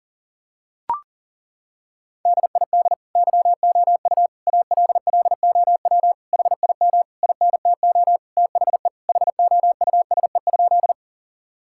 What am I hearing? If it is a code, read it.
Morse code, heard as a keyed tone, about 40 wpm, 700 Hz. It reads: DID YOU ALLOW HIM INTO THE HOUSE?
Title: Sentences from Top 1000 Words - Review Part 5 - 40wpm